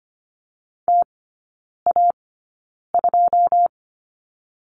Morse code, heard as a keyed tone, about 25 wpm, spaced as for 9 wpm, 690 Hz. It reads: TA2
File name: Lesson 15e - Number 5 - Three Characters 25wpm